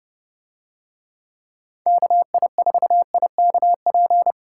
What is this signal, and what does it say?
Morse code, heard as a keyed tone, about 29 wpm, 700 Hz. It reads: KI4IKP